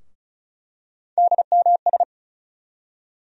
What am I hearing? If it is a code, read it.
Morse code, heard as a keyed tone, about 36 wpm, 700 Hz. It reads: DMS
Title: USA Call Sign Postfixes - 3 Letters 35wpm